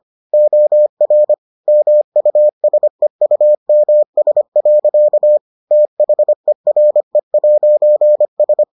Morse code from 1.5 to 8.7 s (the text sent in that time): MUSEUMS. THERE'S